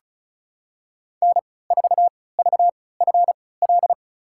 Morse code, heard as a keyed tone, about 36 wpm, spaced as for 20 wpm, 700 Hz. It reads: N4VFL
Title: USA Call Signs - 3x Character Spacing - 35wpm